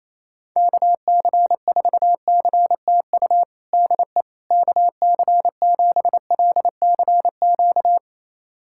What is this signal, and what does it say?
Morse code, heard as a keyed tone, about 28 wpm, 700 Hz. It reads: KC4CTU DE KC7LCQ
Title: USA Callsign DE Callsign - 6 Characters - 28wpm